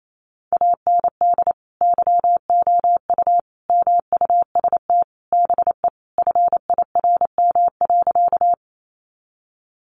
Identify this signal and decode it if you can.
Morse code, heard as a keyed tone, about 28 wpm, 700 Hz. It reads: AND YOU MUST BE FIRM.
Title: Sentences from Top 1400 Words 28wpm